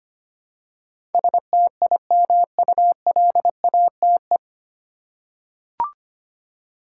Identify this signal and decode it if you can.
Morse code, heard as a keyed tone, about 25 wpm, 700 Hz. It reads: STIMULATE
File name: Nine Letter Words - 25wpm